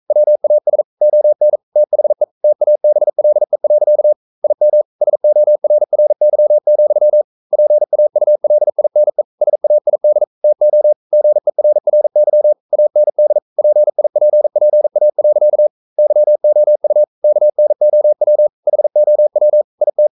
Morse code, heard as a keyed tone, about 42 wpm, 600 Hz. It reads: WAS ON THE TABLE. IM SORRY, PAULINE SAID TO GERRY AND PIPPA. YOU KNOW HOW IT
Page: Romanen Never af Ken Follet på engelsk.